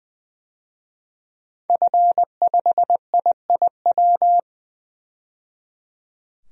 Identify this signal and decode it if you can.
Morse code, heard as a keyed tone, about 20 wpm, 700 Hz. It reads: F5IIW
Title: DXCC-100 Call Signs - 5 Characters - 20wpm